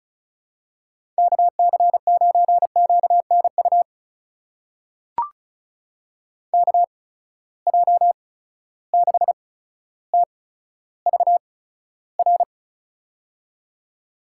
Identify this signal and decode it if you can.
Morse code, heard as a keyed tone, about 36 wpm, spaced as for 15 wpm, 700 Hz. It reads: KC9QNU KJ6TVR
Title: USA Call Signs - 6 Characters - 8x Character Spacing - 35wpm